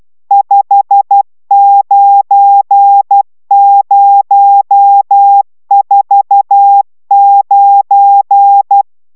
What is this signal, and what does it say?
Morse code, heard as a keyed tone, about 12 wpm, 800 Hz. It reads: 59049